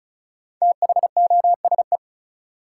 Morse code, heard as a keyed tone, about 36 wpm, 700 Hz. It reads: THOSE
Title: Sentences from Top 300 Words - New Word Drill - 35wpm